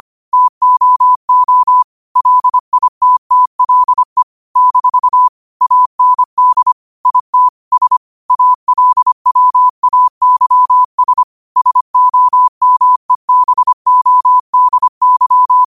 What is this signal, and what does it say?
Morse code, heard as a keyed tone, about 25 wpm, 1000 Hz. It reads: TOO LITTLE - AND IT S ALWAYS SOMEBODY
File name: qotdfiles / 25wpm